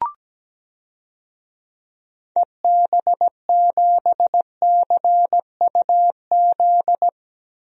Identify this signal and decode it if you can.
Morse code, heard as a keyed tone, about 17 wpm, 700 Hz. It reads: EB7CUZ
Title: DXCC-100 Call Signs - 6 Characters - Repeated 2x - 17wpm